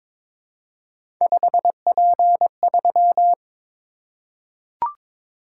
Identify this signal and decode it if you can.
Morse code, heard as a keyed tone, about 22 wpm, 700 Hz. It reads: 5P3
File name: practice / 22wpm / Three Letter-Number Combinations - 22wpm